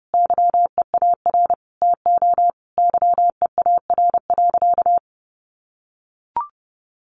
Morse code, heard as a keyed tone, about 28 wpm, 700 Hz. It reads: YEAR TO YEAR.
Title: Sentences from Top 1200 Words 30wpm